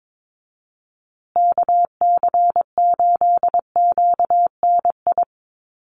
Morse code, heard as a keyed tone, about 22 wpm, 700 Hz. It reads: KC8QNI
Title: USA Call Signs Repeated 2x 22wpm